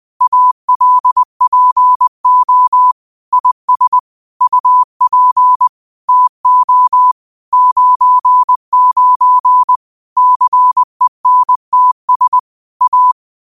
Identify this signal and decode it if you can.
Morse code, heard as a keyed tone, about 20 wpm, 1000 Hz. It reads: ALPO IS UP TO 99 CENTS A
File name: qotdfiles / 20wpm